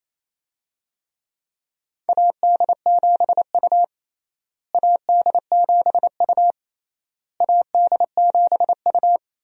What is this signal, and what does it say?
Morse code, heard as a keyed tone, about 28 wpm, 700 Hz. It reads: AD7U AD7U AD7U
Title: USA Call Signs - 4 Characters - Repeated 3x - 28wpm